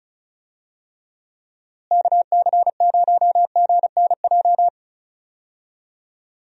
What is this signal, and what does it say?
Morse code, heard as a keyed tone, about 34 wpm, 700 Hz. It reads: KC0GNJ
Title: USA Call Signs - 6 Characters - Repeated 3x - 35wpm